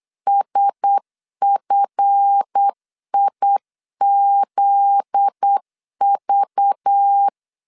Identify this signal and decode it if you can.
Morse code, heard as a keyed tone, about 8 wpm, 800 Hz. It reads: SFIZV
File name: LEZIONI / 7 GF9Z / 7 GF9Z 8wpm 3213